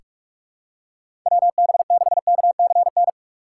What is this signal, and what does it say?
Morse code, heard as a keyed tone, about 44 wpm, 700 Hz. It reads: WB6KCN
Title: USA Call Signs - 6x Character Spacing - 45wpm